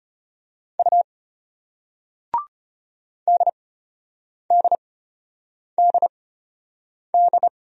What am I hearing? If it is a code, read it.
Morse code, heard as a keyed tone, about 33 wpm, spaced as for 8 wpm, 700 Hz. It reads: U DDDD